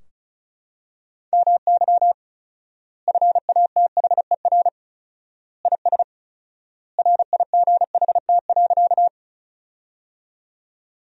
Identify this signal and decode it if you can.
Morse code, heard as a keyed tone, about 36 wpm, 700 Hz. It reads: MY FATHER IS RIGHT.
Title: 4x Word-Spacing - Sentences from Top 300 Words - 35wpm